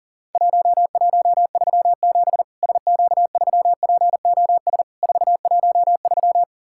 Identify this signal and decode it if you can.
Morse code, heard as a keyed tone, about 40 wpm, 700 Hz. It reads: 1137 SQ3POS 413
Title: Worked All Europe DX Contest - QTC Exchange - 40wpm